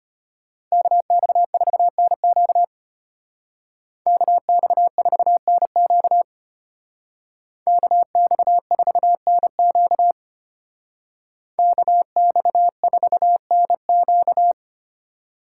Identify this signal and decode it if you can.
Morse code, heard as a keyed tone, about 33 wpm, 700 Hz. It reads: KX4NQ KX4NQ KX4NQ KX4NQ